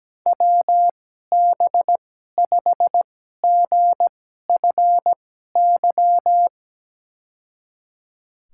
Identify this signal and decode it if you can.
Morse code, heard as a keyed tone, about 17 wpm, spaced as for 12 wpm, 700 Hz. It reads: WB5GFY
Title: USA Call Signs - 6 Characters - 2x Character Spacing - 17wpm